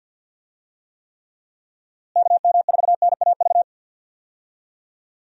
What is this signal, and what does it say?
Morse code, heard as a keyed tone, about 52 wpm, 700 Hz. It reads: KM4NAV
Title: USA Call Signs - 6 Characters - Repeated 3x - 50wpm